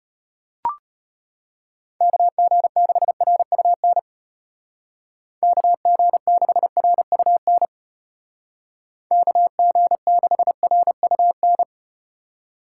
Morse code, heard as a keyed tone, about 34 wpm, 700 Hz. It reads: KG6RUN KG6RUN KG6RUN